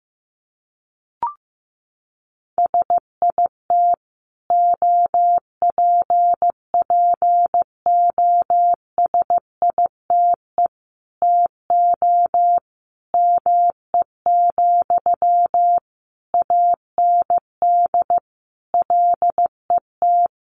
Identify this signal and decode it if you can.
Morse code, heard as a keyed tone, about 15 wpm, 700 Hz. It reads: SIT OPPOSITE TO ME, AND LET
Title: Sentences from Top 1300 Words 15wpm